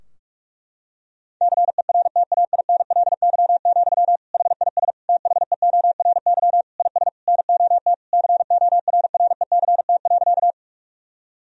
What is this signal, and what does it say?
Morse code, heard as a keyed tone, about 46 wpm, 700 Hz. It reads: CERTAINLY, HIS THEORY IS NOT CORRECT.
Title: Sentences from Top 1000 Words - Review Part 3 - 45wpm